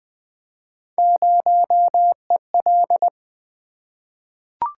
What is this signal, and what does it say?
Morse code, heard as a keyed tone, about 20 wpm, 700 Hz. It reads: 0EL
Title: Three Letter-Number Combinations - 20wpm